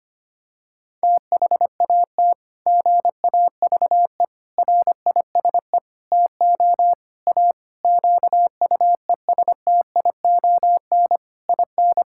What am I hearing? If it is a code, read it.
Morse code, heard as a keyed tone, about 25 wpm, 700 Hz. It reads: THAT GAVE RISE TO A QUESTION IN
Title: Sentences from Top 1000 Words - Review Part 6 - 25wpm